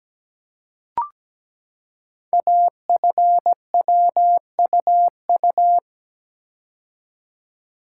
Morse code, heard as a keyed tone, about 17 wpm, 700 Hz. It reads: AFWUU
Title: Code Groups - 5 Letters - 17wpm